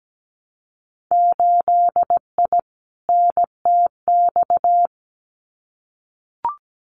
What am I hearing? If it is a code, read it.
Morse code, heard as a keyed tone, about 17 wpm, 700 Hz. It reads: 8I NTX
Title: Winter Field Day Exchanges Encore 17wpm